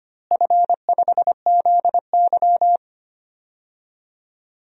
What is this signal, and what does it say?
Morse code, heard as a keyed tone, about 25 wpm, 700 Hz. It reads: F5ZY
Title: DXCC-100 Call Signs - 4 Characters - Repeated 2x - 25wpm